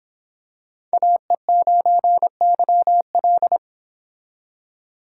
Morse code, heard as a keyed tone, about 26 wpm, 700 Hz. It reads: AE9YL